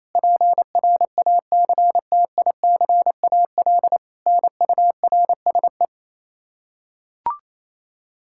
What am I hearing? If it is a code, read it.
Morse code, heard as a keyed tone, about 28 wpm, 700 Hz. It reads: PRACTICAL NURSE
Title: Practice QSO - Exchange 3 - More Info - 28wpm